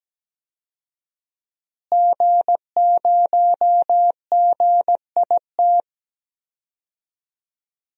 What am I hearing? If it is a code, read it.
Morse code, heard as a keyed tone, about 17 wpm, 700 Hz. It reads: G0GIT
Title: DXCC-100 Call Signs - 5 Characters - Repeated 2x - 17wpm